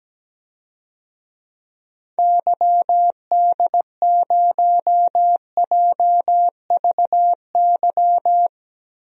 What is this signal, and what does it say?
Morse code, heard as a keyed tone, about 17 wpm, 700 Hz. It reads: YD0JVY
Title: DXCC-100 Call Signs - 6 Characters - Repeated 2x - 17wpm